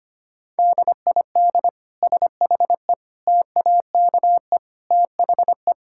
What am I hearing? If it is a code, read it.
Morse code, heard as a keyed tone, about 25 wpm, 700 Hz. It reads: DID SHE TAKE THE
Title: Sentences from Top 1000 Words - Review Part 6 - 25wpm